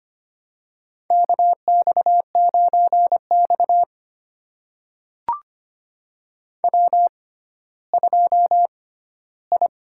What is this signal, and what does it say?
Morse code, heard as a keyed tone, about 25 wpm, spaced as for 14 wpm, 700 Hz. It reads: KX9X W2I
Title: USA Call Signs - 6x Character Spacing - 25wpm